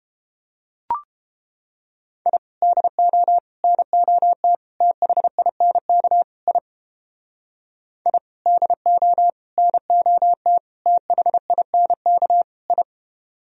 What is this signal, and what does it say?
Morse code, heard as a keyed tone, about 31 wpm, 700 Hz. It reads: I DO NOT THINK I I DO NOT THINK I